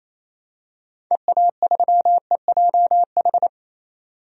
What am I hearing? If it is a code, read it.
Morse code, heard as a keyed tone, about 28 wpm, 700 Hz. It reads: EA3EJH
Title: DXCC-100 Call Signs - 6 Characters - 28wpm